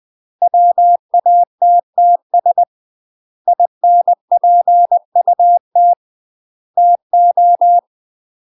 Morse code, heard as a keyed tone, about 20 wpm, 700 Hz. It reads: WATTS INPUT TO
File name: UK MASTER QSOs TRAINING/20WPM